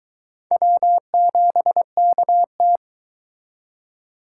Morse code, heard as a keyed tone, about 22 wpm, 700 Hz. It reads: W7KT